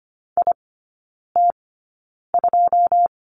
Morse code, heard as a keyed tone, about 25 wpm, spaced as for 9 wpm, 700 Hz. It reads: IT2
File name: course / 25:10wpm / 093 Lesson 16e - Letter c - Three Characters 25wpm